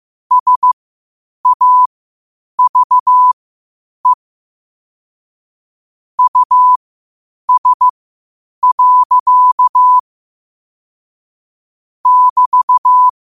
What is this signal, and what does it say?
Morse code, heard as a keyed tone, about 15 wpm, spaced as for 8 wpm, 1000 Hz. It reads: SAVE US. <BT>
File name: qotdfiles / 8wpm